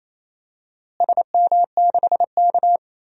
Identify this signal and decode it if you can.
Morse code, heard as a keyed tone, about 28 wpm, 700 Hz. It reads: SM6K
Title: DXCC-100 Call Signs - 4 Characters - Repeated 2x - 28wpm